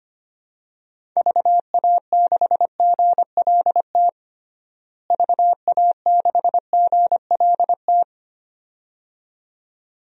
Morse code, heard as a keyed tone, about 25 wpm, 700 Hz. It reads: VA6GLT VA6GLT
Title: DXCC-100 Call Signs - 6 Characters - Repeated 2x - 25wpm